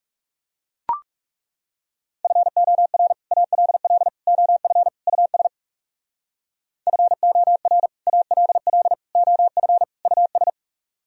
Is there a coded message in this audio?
Yes: Morse code from 2.2 to 10.5 s, heard as a keyed tone, about 40 wpm, 700 Hz. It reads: FOR ALL OF US FOR ALL OF US